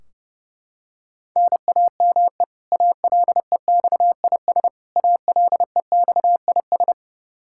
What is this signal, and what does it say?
Morse code, heard as a keyed tone, about 32 wpm, 700 Hz. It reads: NAME ALEXIS ALEXIS
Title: K1USN SST QSO Practice 30wpm